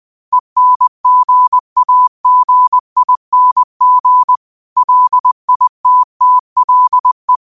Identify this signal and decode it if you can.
Morse code, heard as a keyed tone, about 20 wpm, 1000 Hz. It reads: ENGAGING LITTLE
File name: qotdfiles / 20wpm